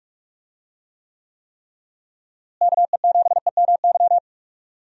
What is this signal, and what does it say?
Morse code, heard as a keyed tone, about 43 wpm, 700 Hz. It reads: KE7EMY
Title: USA Call Signs - 6 Characters - Repeated 3x - 45wpm